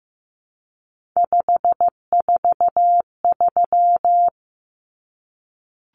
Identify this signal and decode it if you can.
Morse code, heard as a keyed tone, about 15 wpm, 700 Hz. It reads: 543
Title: Numbers - 3-Digit 15wpm